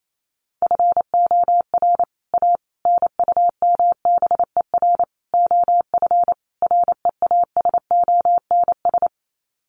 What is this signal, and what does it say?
Morse code, heard as a keyed tone, about 28 wpm, 700 Hz. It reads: FOR A NUMBER OF REASONS
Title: Sets of 5 Words Encore2 28wpm